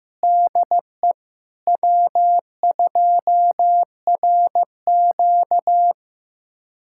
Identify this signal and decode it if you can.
Morse code, heard as a keyed tone, about 15 wpm, 700 Hz. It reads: DE W2RQ
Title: USA Callsign DE Callsign - 4 Characters - 15wpm